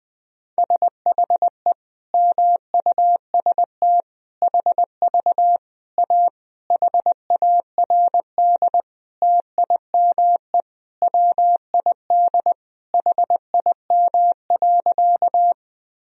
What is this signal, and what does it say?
Morse code, heard as a keyed tone, about 20 wpm, 700 Hz. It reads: SHE MUST HV A HARD TIME WID HIM.
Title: Sentences from Top 300 Words and Abbreviations 20wpm